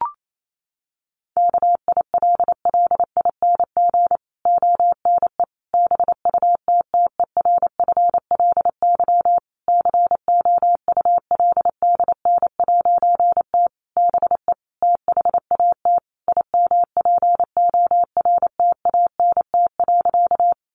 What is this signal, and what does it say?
Morse code, heard as a keyed tone, about 28 wpm, 700 Hz. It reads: KILLING ONE BUTTERFLY COULDN'T BE THAT IMPORTANT.
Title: A Sound Of Thunder 28wpm